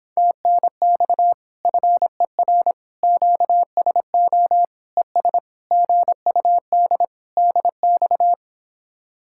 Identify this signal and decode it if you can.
Morse code, heard as a keyed tone, about 26 wpm, 700 Hz. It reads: TNX FER QSO ES GUD DX